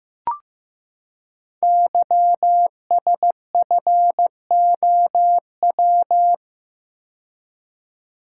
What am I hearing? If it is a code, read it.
Morse code, heard as a keyed tone, about 15 wpm, 700 Hz. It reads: YSFOW
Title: Code Groups - 5 Letter:Numbers - 15wpm